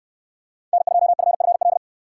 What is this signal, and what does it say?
Morse code, heard as a keyed tone, about 68 wpm, 700 Hz. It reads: N2FFL